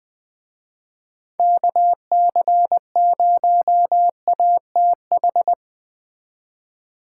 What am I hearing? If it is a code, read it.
Morse code, heard as a keyed tone, about 20 wpm, 700 Hz. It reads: KC0ATH